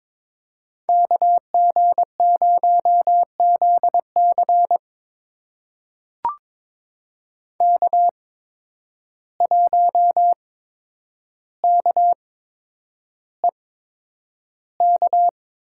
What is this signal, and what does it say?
Morse code, heard as a keyed tone, about 22 wpm, spaced as for 9 wpm, 700 Hz. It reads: KG0ZC K1KEK
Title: USA Call Signs - 5 Characters - 8x Character Spacing - 22wpm